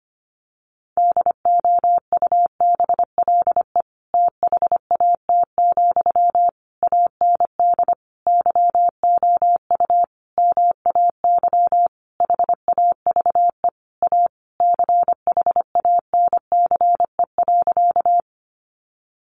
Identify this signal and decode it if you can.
Morse code, heard as a keyed tone, about 25 wpm, 700 Hz. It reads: DOUBLE THAT, AND YOU MAY HAVE A CHANCE.
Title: Sentences from Top 1400 Words 25wpm